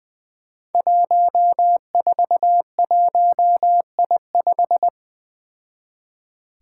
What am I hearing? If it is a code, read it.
Morse code, heard as a keyed tone, about 20 wpm, 700 Hz. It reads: 141I5
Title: Five Challenging Letter-Number Combinations 20wpm